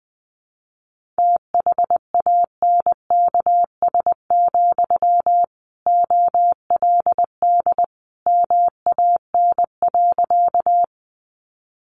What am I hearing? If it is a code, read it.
Morse code, heard as a keyed tone, about 20 wpm, 700 Hz. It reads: THANKS, OLD MAN.
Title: Sentences from Top 600 Words 20wpm